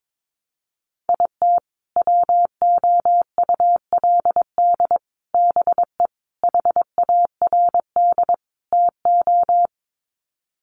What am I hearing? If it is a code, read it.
Morse code, heard as a keyed tone, about 22 wpm, 700 Hz. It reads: IT WOULD BE HARD TO
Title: Sets of 5 Words Encore2 22wpm